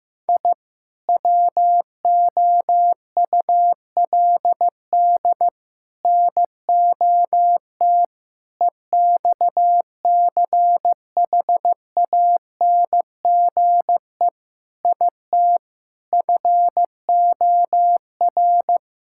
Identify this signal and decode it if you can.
Morse code, heard as a keyed tone, about 15 wpm, 700 Hz. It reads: I WOULD NOT EXCHANGE IT FOR
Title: Sentences from Top 1000 Words - Review Part 6 - 15wpm